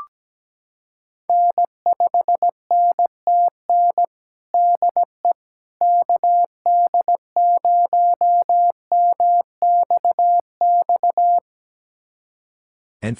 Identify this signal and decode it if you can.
Morse code, heard as a keyed tone, about 17 wpm, 700 Hz. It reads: N5NTN DE KD0MXX
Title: USA Callsign DE Callsign 17wpm